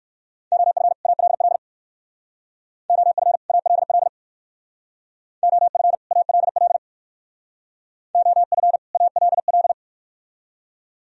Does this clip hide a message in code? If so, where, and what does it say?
Yes: Morse code from 0.5 to 9.7 s, heard as a keyed tone, about 58 wpm, 690 Hz. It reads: OF ALL OF ALL OF ALL OF ALL